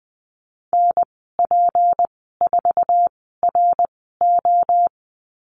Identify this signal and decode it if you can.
Morse code, heard as a keyed tone, about 20 wpm, spaced as for 15 wpm, 700 Hz. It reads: NP4RO
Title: 186 Lesson 30g - Letter b - Call Signs 20wpm